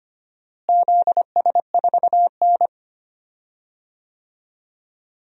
Morse code, heard as a keyed tone, about 25 wpm, 700 Hz. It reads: ZS4N
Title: DXCC-100 Call Signs - 4 Characters - Repeated 3x - 25wpm